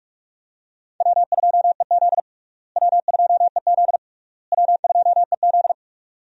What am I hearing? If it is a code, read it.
Morse code, heard as a keyed tone, about 46 wpm, 700 Hz. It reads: W2EZ W2EZ W2EZ
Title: USA Call Signs - 4 Characters - Repeated 3x - 45wpm